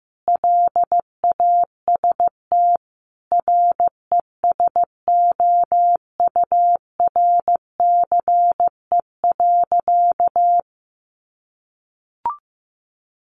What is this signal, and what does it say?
Morse code, heard as a keyed tone, about 15 wpm, 700 Hz. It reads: LAST RESOURCE.
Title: Sentences from Top 1300 Words 15wpm